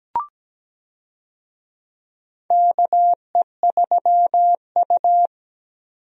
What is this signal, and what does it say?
Morse code, heard as a keyed tone, about 17 wpm, 700 Hz. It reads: KE3U
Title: USA Call Signs - 4 Characters - 17wpm